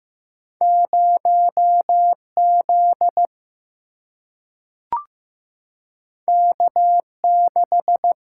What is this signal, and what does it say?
Morse code, heard as a keyed tone, about 15 wpm, 700 Hz. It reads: 0 Z K 6
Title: Single-Letter Single-Number 15wpm